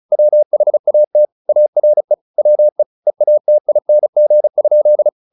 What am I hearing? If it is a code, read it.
Morse code, heard as a keyed tone, about 34 wpm, 600 Hz. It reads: WHAT ARE WE EATING?
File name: the old man and the sea / the old man and the sea 1 35wpm